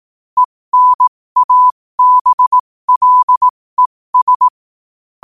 Morse code, heard as a keyed tone, about 18 wpm, spaced as for 15 wpm, 1000 Hz. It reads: ENABLES